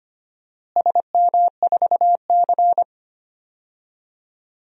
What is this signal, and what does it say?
Morse code, heard as a keyed tone, about 25 wpm, 700 Hz. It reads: SM4C
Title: DXCC-100 Call Signs - 4 Characters - Repeated 2x - 25wpm